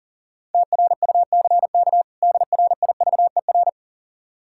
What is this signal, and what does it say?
Morse code, heard as a keyed tone, about 40 wpm, 700 Hz. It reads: TRUCK DRIVER
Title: Practice QSO - Exchange 3 - More Info - 40wpm